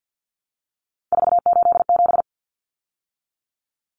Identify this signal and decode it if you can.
Morse code, heard as a keyed tone, about 50 wpm, 700 Hz. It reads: 487
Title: Common Cut Numbers - 3-Digit 50wpm